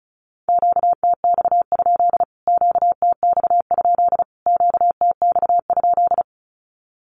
Morse code, heard as a keyed tone, about 34 wpm, 700 Hz. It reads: QTX? QTX? QTX?
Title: Q-Codes Repeated 3x 35wpm